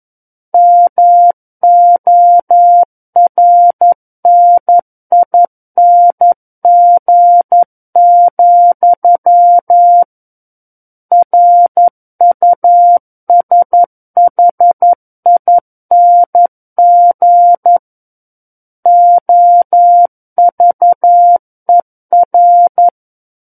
Morse code, heard as a keyed tone, about 11 wpm, 700 Hz. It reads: MORNING, RUSHING OVER
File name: 11 WPM